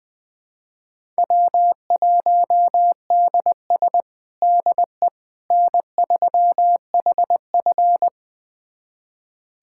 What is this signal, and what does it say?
Morse code, heard as a keyed tone, about 21 wpm, 700 Hz. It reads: W1DS DE N3HF
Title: USA Callsign DE Callsign - 4 Characters - 20wpm